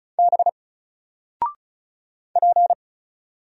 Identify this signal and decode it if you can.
Morse code, heard as a keyed tone, about 34 wpm, 700 Hz. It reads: B P
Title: Single Letter-Number Alt 35wpm